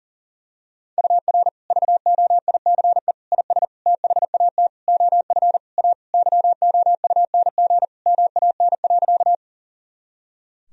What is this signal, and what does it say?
Morse code, heard as a keyed tone, about 40 wpm, 700 Hz. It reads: UR VOICE IS THAT OF A YOUNG MAN.
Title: Sentences from Top 500 Words and Abbreviations 40wpm